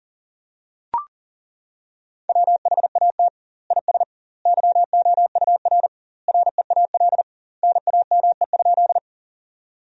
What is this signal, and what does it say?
Morse code, heard as a keyed tone, about 40 wpm, 700 Hz. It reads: WHAT IS YOUR REAL NAME?
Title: 2x Word-Spacing - Sentences from Top 300 Words - 40wpm